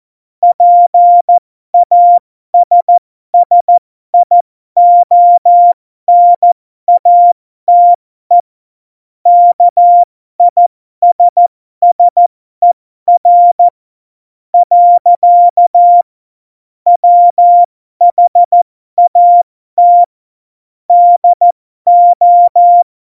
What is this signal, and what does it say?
Morse code, heard as a keyed tone, about 14 wpm, spaced as for 12 wpm, 700 Hz. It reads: PASSIONATE KISSER. WHAT DO
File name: Morsekurs / Witze